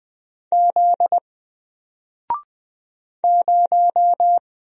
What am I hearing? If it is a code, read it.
Morse code, heard as a keyed tone, about 20 wpm, 700 Hz. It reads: Z 0
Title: Single Letter-Number 20wpm